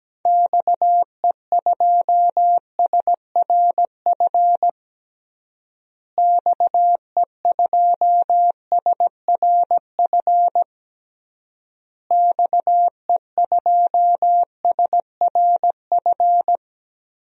Morse code, heard as a keyed tone, about 17 wpm, 700 Hz. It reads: XE2SRF XE2SRF XE2SRF
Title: DXCC-100 Call Signs - 6 Characters - Repeated 3x - 17wpm